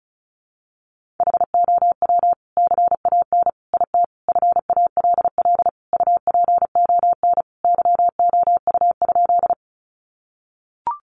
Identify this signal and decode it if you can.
Morse code, heard as a keyed tone, about 34 wpm, 700 Hz. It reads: HOW CAN IT FALL UPON YOU?
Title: Sentences from Top 1100 Words 35wpm